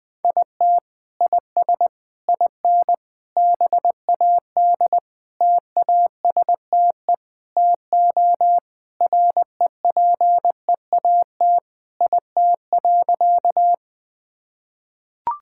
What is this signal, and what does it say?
Morse code, heard as a keyed tone, about 20 wpm, 700 Hz. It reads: IT IS IN BAD TASTE TO REPEAT IT.
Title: Sentences from Top 1600 Words 20wpm